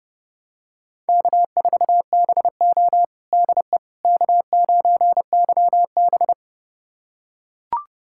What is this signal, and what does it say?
Morse code, heard as a keyed tone, about 30 wpm, 700 Hz. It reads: K4BO DE K9YB
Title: USA Callsign DE Callsign - 4 Characters - 30wpm